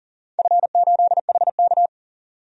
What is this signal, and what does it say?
Morse code, heard as a keyed tone, about 40 wpm, 700 Hz. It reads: F8HK
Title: DXCC-100 Call Signs - 4 Characters - 40wpm